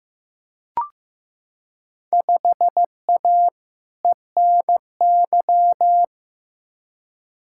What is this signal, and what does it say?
Morse code, heard as a keyed tone, about 15 wpm, 700 Hz. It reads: 5A ENY
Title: Field Day Exchanges - 15wpm